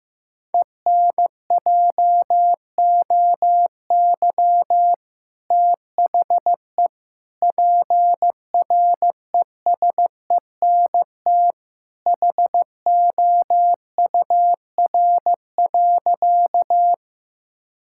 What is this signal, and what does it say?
Morse code, heard as a keyed tone, about 15 wpm, 690 Hz. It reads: ENJOY THE PRESENT HOUR.
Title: Sentences from Top 1000 Words - Review Part 5 - 15wpm